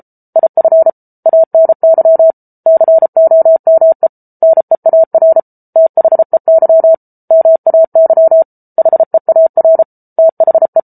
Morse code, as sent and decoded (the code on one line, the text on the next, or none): .. ..-. / .- -. -.-- / -.-. --- -- . / -. .- .-. / - .... -.-- / -- .- -.-- / .... .- .-. / - .... .
IF ANY COME NEAR THEY MAY HEAR THE